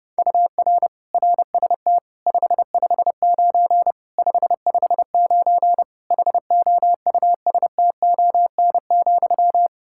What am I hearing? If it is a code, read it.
Morse code, heard as a keyed tone, about 30 wpm, 700 Hz. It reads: UR RST 559 559 HOUSTON,
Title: K1USN SST QSO Practice 30wpm